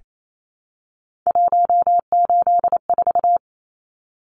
Morse code, heard as a keyed tone, about 28 wpm, 700 Hz. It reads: 184
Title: Numbers - 3-Digit 28wpm